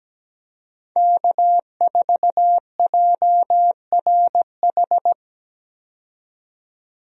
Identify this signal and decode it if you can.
Morse code, heard as a keyed tone, about 17 wpm, 700 Hz. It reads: K4JRH
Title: USA Call Signs - 5 Characters - 17wpm